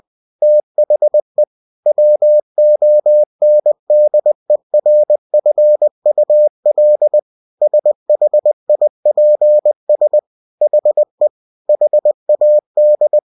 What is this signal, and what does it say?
Morse code, heard as a keyed tone, about 20 wpm, 600 Hz. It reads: THE WONDERFUL SHIPS HE HAD
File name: cw-books / winnie-the-pooh / winnie-20wpm-page-0103